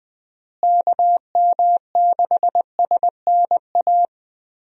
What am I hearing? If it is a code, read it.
Morse code, heard as a keyed tone, about 20 wpm, 700 Hz. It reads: KM6SNA
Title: USA Call Signs - 6 Characters - Repeated 3x - 20wpm